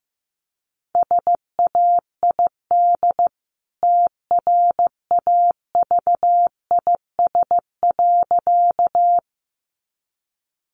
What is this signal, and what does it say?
Morse code, heard as a keyed tone, about 15 wpm, 700 Hz. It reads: SAID TRAVIS.
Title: A Sound Of Thunder 15wpm